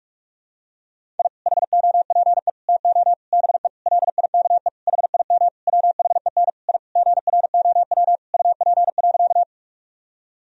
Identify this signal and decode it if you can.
Morse code, heard as a keyed tone, about 42 wpm, 700 Hz. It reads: I HOPE TO BE LIKE HIM WHEN I GROW UP.
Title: Sentences from Top 1000 Words - Review Part 1 - 45wpm